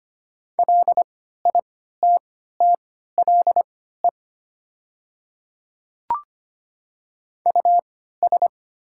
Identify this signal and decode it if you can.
Morse code, heard as a keyed tone, about 25 wpm, spaced as for 14 wpm, 700 Hz. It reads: LITTLE US
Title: 3x Character Spacing - Top 200 Words - 25wpm